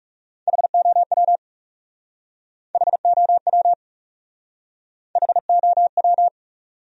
Morse code, heard as a keyed tone, about 39 wpm, 700 Hz. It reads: HOW HOW HOW